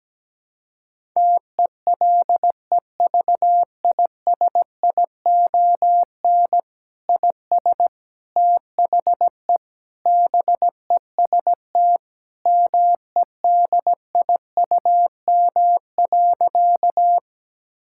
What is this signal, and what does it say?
Morse code, heard as a keyed tone, about 17 wpm, 700 Hz. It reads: TELEVISION IS THE BEST MEDIUM.
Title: Sentences from Top 1000 Words - Review Part 6 - 17wpm